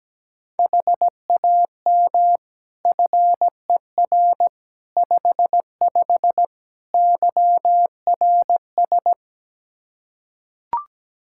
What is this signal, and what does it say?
Morse code, heard as a keyed tone, about 17 wpm, 700 Hz. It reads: HAM FER 55 YRS
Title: Practice QSO - Exchange 3 - More Info - 17wpm